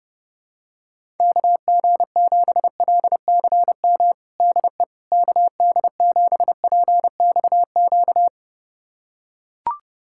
Morse code, heard as a keyed tone, about 30 wpm, 700 Hz. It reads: KG7LCM DE KD7PXQ
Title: USA Callsign DE Callsign - 6 Characters - 30wpm